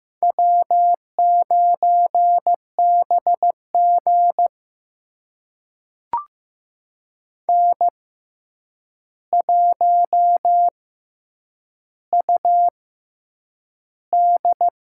Morse code, heard as a keyed tone, about 15 wpm, spaced as for 8 wpm, 700 Hz. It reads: W9BG N1UD
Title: USA Call Signs - 4 Characters - 6x Character Spacing - 15wpm